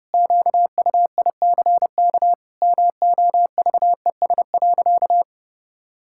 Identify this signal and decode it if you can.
Morse code, heard as a keyed tone, about 30 wpm, 700 Hz. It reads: QUICK MOVES.
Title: Pangram Challenge v2 - 30wpm